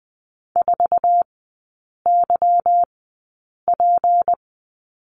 Morse code, heard as a keyed tone, about 20 wpm, spaced as for 8 wpm, 690 Hz. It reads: 4YP
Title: 198 Lesson 32e - Letter k - Three Characters 20wpm